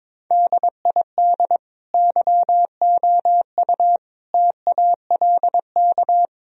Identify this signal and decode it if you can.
Morse code, heard as a keyed tone, about 22 wpm, 700 Hz. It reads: DID YOU TALK
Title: Sentences from Top 1000 Words - Review Part 4 - 22wpm